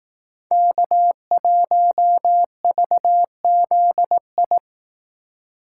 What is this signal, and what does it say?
Morse code, heard as a keyed tone, about 19 wpm, 700 Hz. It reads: K1VZI
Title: Speed Racing - USA 1x3 Call Signs 15wpm